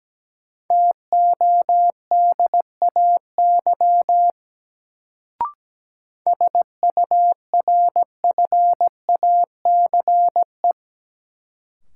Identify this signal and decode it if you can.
Morse code, heard as a keyed tone, about 17 wpm, 700 Hz. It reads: TODAY SURFACE
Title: Sentences from Top 600 Words - New Word Drill - 17wpm